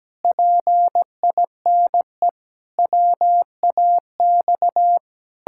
Morse code, heard as a keyed tone, about 17 wpm, 700 Hz. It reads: PINE WAX
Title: Pangram Challenge v2 - 17wpm